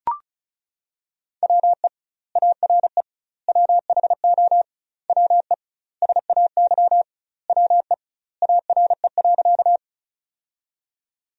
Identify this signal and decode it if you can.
Morse code, heard as a keyed tone, about 36 wpm, 700 Hz. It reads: WE ARE WHO WE SAY WE ARE.
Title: 2x Word-Spacing - Sentences from Top 200 Words - 35wpm